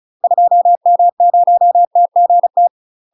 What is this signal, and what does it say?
Morse code, heard as a keyed tone, about 34 wpm, 700 Hz. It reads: 2M0TGT
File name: UK MASTER QSOs TRAINING / 35WPM / UK MASTER QSOs 0034